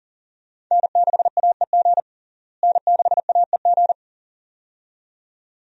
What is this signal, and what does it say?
Morse code, heard as a keyed tone, about 40 wpm, 700 Hz. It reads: N6AEG N6AEG
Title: USA Call Signs - 5 Characters - Repeated 2x - 40wpm